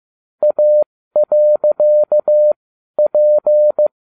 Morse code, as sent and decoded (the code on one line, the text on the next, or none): .- .-.-.- .--.
A.P